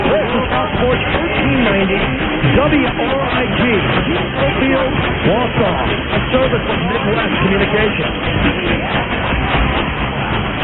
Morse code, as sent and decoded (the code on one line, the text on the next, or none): none